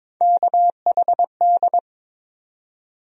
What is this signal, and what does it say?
Morse code, heard as a keyed tone, about 22 wpm, 700 Hz. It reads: KHD
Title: USA Call Sign Postfixes - 3 Letters 22wpm